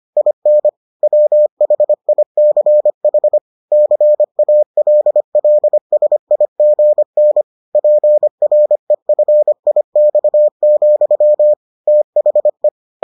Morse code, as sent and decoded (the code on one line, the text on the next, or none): .. -. / .-- .... .. -.-. .... / -.-. .- .-.. .-.. ... .. --. -. / .--. .-. ..-. .. -..- --..-- / - .... .
IN WHICH CALLSIGN PREFIX, THE